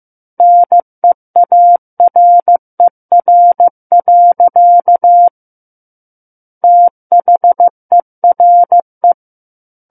Morse code, heard as a keyed tone, about 15 wpm, 700 Hz. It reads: NEARER. THERE
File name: Book One / 06-Chapter 6 - 15 WPM